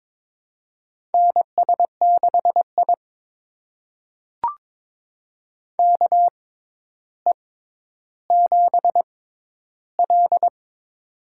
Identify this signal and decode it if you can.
Morse code, heard as a keyed tone, about 22 wpm, spaced as for 11 wpm, 700 Hz. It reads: NS6I KE7L
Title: USA Call Signs - 6x Character Spacing - 22wpm